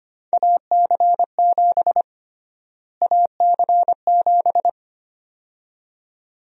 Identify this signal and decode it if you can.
Morse code, heard as a keyed tone, about 25 wpm, 700 Hz. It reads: AC7 AC7
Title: USA Call Sign Prefixes - Repeated 2x - 25wpm